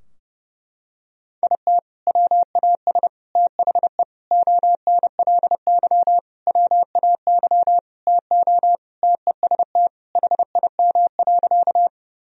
Morse code, heard as a keyed tone, about 29 wpm, 700 Hz. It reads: IT WAS THE ONLY WAY TO TEST HIM.
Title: Sentences from Top 1000 Words - Review Part 3 - 30wpm